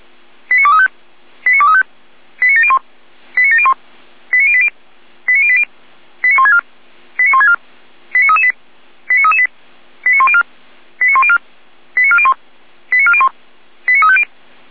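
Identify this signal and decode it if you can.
Five-tone selective call (ZVEI), heard as quick runs of five tones, tones of about 70 ms. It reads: 89235 89235 89701 89701 89080 89080 89164 89164 89338 89338 89104 89104 89401 89401 89350
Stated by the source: ZVEI five-tone selective call received by SDR.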